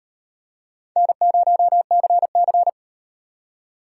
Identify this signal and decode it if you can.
Morse code, heard as a keyed tone, about 33 wpm, 700 Hz. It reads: N0CC